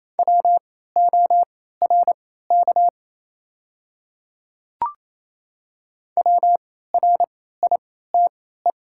Morse code, heard as a keyed tone, about 28 wpm, spaced as for 16 wpm, 700 Hz. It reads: WORK WRITE
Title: 3x Character Spacing - Top 200 Words - 28wpm